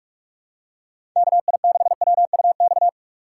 Morse code, heard as a keyed tone, about 46 wpm, 700 Hz. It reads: KI6WUX
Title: USA Call Signs - 2x Character Spacing - 45wpm